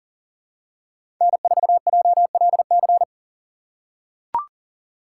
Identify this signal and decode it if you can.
Morse code, heard as a keyed tone, about 40 wpm, 700 Hz. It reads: N4JLC
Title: USA Call Signs - 5 Characters - 2x Character Spacing - 40wpm